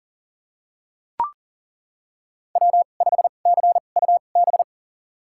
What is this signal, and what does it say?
Morse code, heard as a keyed tone, about 40 wpm, spaced as for 29 wpm, 700 Hz. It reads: W5CUB
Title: USA Call Signs - 5 Characters - 2x Character Spacing - 40wpm